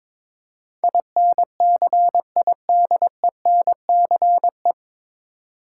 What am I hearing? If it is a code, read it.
Morse code, heard as a keyed tone, about 22 wpm, 700 Hz. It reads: INCIDENCE
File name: 9 Letter Words - First 3 Letters Given - 22wpm